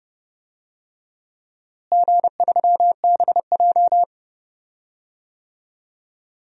Morse code, heard as a keyed tone, about 31 wpm, 700 Hz. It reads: G3BJ
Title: DXCC-100 Call Signs - 4 Characters - Repeated 2x - 30wpm